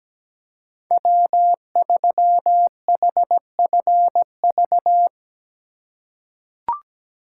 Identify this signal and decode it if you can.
Morse code, heard as a keyed tone, about 17 wpm, 700 Hz. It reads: W3HFV
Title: USA Call Signs - 5 Characters - 3x Character Spacing - 17wpm